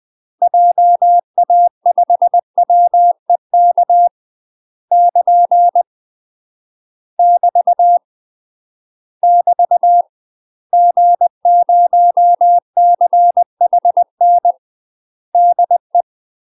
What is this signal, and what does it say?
Morse code, heard as a keyed tone, about 20 wpm, 700 Hz. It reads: JA5WEK <KN> <BT> <BT> G0CHN DE
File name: UK MASTER QSOs TRAINING / 20WPM / UK MASTER QSOs 0023